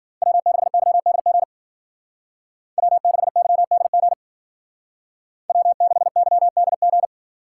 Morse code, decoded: W6YDG W6YDG W6YDG

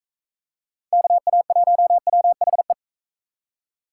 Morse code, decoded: KA1WHE

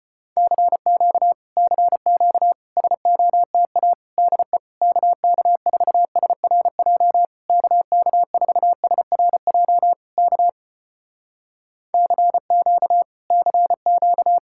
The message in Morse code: CQ CQ SOTA DE KK4SRJ KK4SRJ K CQ CQ

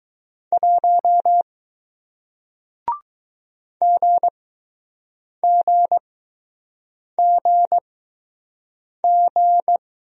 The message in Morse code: 1 GGGG

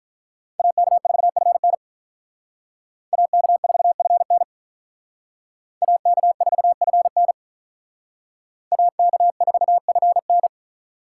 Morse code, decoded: AK4FN AK4FN AK4FN AK4FN

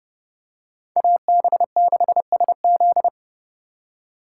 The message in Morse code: AB6SZ

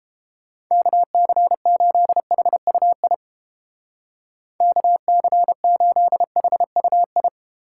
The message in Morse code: KC8HUI KC8HUI

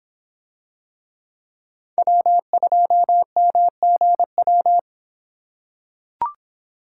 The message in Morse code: W2MGW

